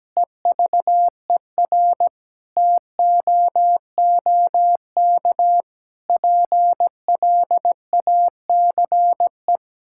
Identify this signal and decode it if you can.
Morse code, heard as a keyed tone, about 17 wpm, 700 Hz. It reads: EVER TOOK PLACE
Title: Sentences from Top 1000 Words - Review Part 6 - 17wpm